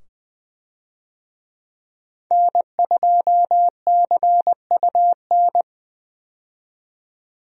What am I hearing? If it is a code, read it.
Morse code, heard as a keyed tone, about 21 wpm, 700 Hz. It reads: N2CUN
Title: USA Call Signs - 5 Characters - 20wpm